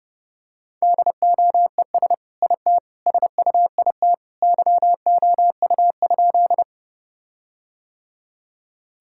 Morse code, heard as a keyed tone, about 32 wpm, 700 Hz. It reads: DOES IT SUIT YOU?
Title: Sentences from Top 1600 Words 30wpm